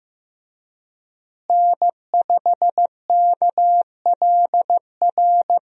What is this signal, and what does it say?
Morse code, heard as a keyed tone, about 15 wpm, 700 Hz. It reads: N5KLR
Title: USA Call Signs - 5 Characters - Repeated 3x - 15wpm